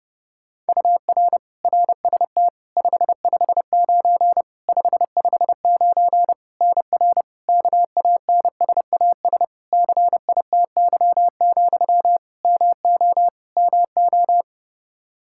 Morse code, heard as a keyed tone, about 30 wpm, 700 Hz. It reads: UR RST 559 559 NR KANSAS CITY, MO MO
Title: K1USN SST QSO Practice 30wpm